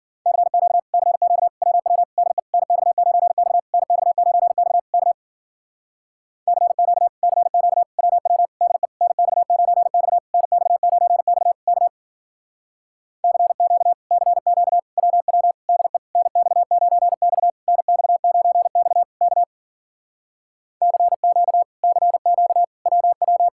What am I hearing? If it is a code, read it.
Morse code, heard as a keyed tone, about 51 wpm, 700 Hz. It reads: CQ CQ WW DE NX9X NX9X K CQ CQ WW DE NX9X NX9X K CQ CQ WW DE NX9X NX9X K CQ CQ WW